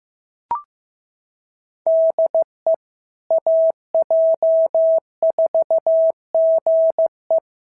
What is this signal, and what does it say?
Morse code, heard as a keyed tone, about 15 wpm, 650 Hz. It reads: DE AJ4GE
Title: SOTA QSO Practice 15wpm